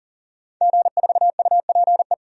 Morse code, heard as a keyed tone, about 40 wpm, 700 Hz. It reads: G4UPE
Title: DXCC-100 Call Signs - 5 Characters - 40wpm